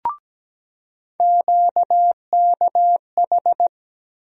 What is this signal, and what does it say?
Morse code, heard as a keyed tone, about 17 wpm, 700 Hz. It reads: QKH
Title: USA Call Sign Postfixes - 3 Letters 17wpm